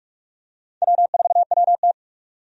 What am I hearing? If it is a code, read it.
Morse code, heard as a keyed tone, about 41 wpm, 700 Hz. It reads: W4WT
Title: USA Call Signs - 4 Characters - 45wpm